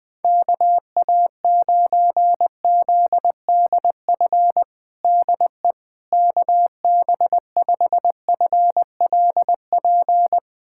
Morse code, heard as a keyed tone, about 20 wpm, 700 Hz. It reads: KA9ZDF DE KB5FLP
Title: USA Callsign DE Callsign - 6 Characters - 20wpm